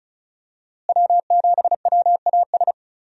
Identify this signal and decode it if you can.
Morse code, heard as a keyed tone, about 36 wpm, 700 Hz. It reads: W7WAS
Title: USA Call Signs - 5 Characters - Repeated 3x - 35wpm